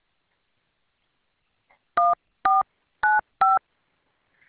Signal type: DTMF